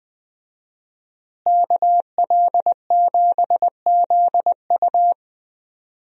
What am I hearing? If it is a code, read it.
Morse code, heard as a keyed tone, about 21 wpm, 700 Hz. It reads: KL7ZU
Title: USA Call Signs - 5 Characters - Repeated 2x - 20wpm